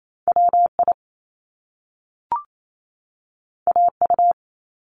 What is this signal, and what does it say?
Morse code, heard as a keyed tone, about 28 wpm, 700 Hz. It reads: WI AU
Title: Top Two Letter Combinations in Words 28wpm